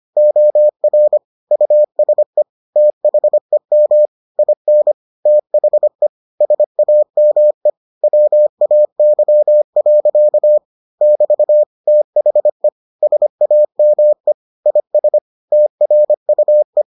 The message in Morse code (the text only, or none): OR USE THEM IN THE SAME WAY. <BT> THE SAME IS TRUE